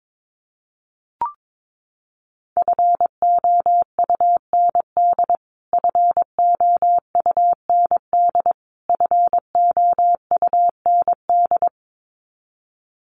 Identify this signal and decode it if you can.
Morse code, heard as a keyed tone, about 22 wpm, 700 Hz. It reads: FOUND FOUND FOUND
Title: Top 300 Words Repeated 22wpm